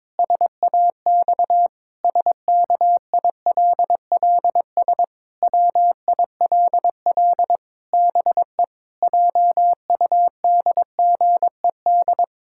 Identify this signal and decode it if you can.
Morse code, heard as a keyed tone, about 22 wpm, 700 Hz. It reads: SAX SKILLS WILL BE JUDGED
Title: Pangram Challenge v2 - 22wpm